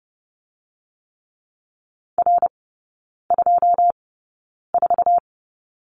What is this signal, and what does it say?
Morse code, heard as a keyed tone, about 30 wpm, 700 Hz. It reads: R 2 4
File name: 081 Lesson 14e - Number 2 - Three Characters 30wpm